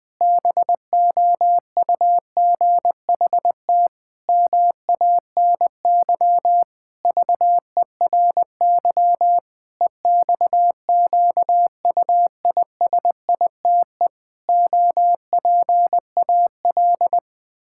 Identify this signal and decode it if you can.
Morse code, heard as a keyed tone, about 20 wpm, 700 Hz. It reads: BOUGHT MANY VERY EXQUISITE OPAL
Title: Pangram Challenge v2 - 20wpm